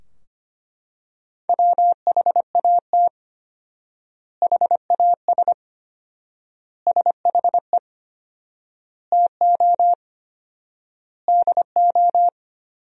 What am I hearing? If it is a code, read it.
Morse code, heard as a keyed tone, about 25 wpm, 700 Hz. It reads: WHAT HAS SHE TO DO
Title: 4x Word-Spacing - Sentences from Top 100 Words - 25wpm